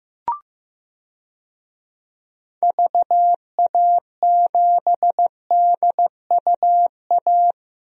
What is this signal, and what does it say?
Morse code, heard as a keyed tone, about 15 wpm, 700 Hz. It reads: VA7DUA
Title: DXCC-100 Call Signs - 6 Characters - Repeated 2x - 15wpm